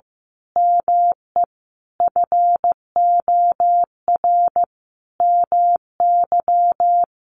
Morse code, heard as a keyed tone, about 15 wpm, 700 Hz. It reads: ME FOR MY
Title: Sentences from Top 1200 Words 15wpm